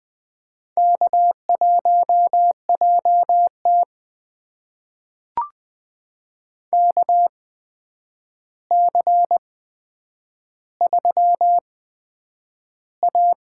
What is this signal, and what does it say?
Morse code, heard as a keyed tone, about 20 wpm, spaced as for 9 wpm, 700 Hz. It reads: K1JT KC3A
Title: USA Call Signs - 4 Characters - 8x Character Spacing - 20wpm